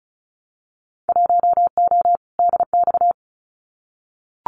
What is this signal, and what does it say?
Morse code, heard as a keyed tone, about 36 wpm, 700 Hz. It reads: 1O DX
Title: Winter Field Day Exchanges 35wpm